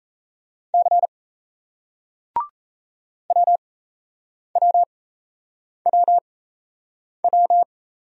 Morse code, heard as a keyed tone, about 38 wpm, spaced as for 8 wpm, 700 Hz. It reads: C WWWW